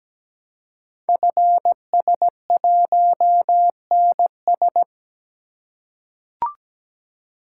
Morse code, heard as a keyed tone, about 17 wpm, 700 Hz. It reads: FS1NS
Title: Code Groups - 5 Letter:Numbers - 17wpm